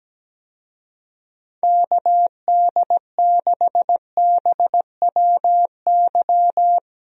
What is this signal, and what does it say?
Morse code, heard as a keyed tone, about 17 wpm, 700 Hz. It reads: KD6BWY
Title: USA Call Signs - 6 Characters - 17wpm